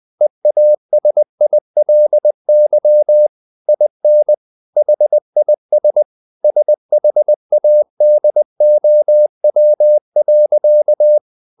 Morse code, heard as a keyed tone, about 20 wpm, 600 Hz. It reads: EASILY IN HIS SHADOW.
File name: the old man and the sea part 2 20wpm